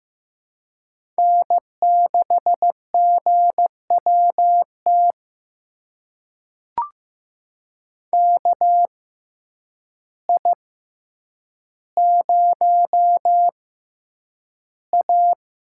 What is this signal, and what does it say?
Morse code, heard as a keyed tone, about 15 wpm, spaced as for 8 wpm, 700 Hz. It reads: N6GWT KI0A
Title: USA Call Signs - 5 Characters - 6x Character Spacing - 15wpm